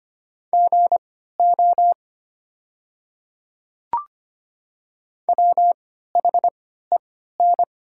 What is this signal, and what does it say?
Morse code, heard as a keyed tone, about 25 wpm, spaced as for 14 wpm, 700 Hz. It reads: GO WHEN
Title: 3x Character Spacing - Top 100 Words - 25wpm